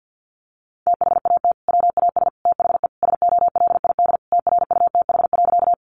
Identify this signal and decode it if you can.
Morse code, heard as a keyed tone, about 50 wpm, 700 Hz. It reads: THAT WAS THE SOLID TRUTH.
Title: Sentences from Top 1500 Words 50wpm